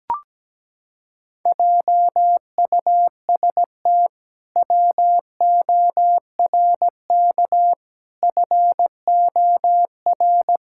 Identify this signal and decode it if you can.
Morse code, heard as a keyed tone, about 17 wpm, 700 Hz. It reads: JUST WORK FOR
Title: Pangram Challenge v2 - 17wpm